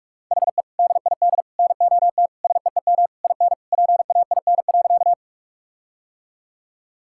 Morse code, heard as a keyed tone, about 46 wpm, 700 Hz. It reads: HE DID NOT SEEM IN PAIN.
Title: Sentences from Top 1600 Words 45wpm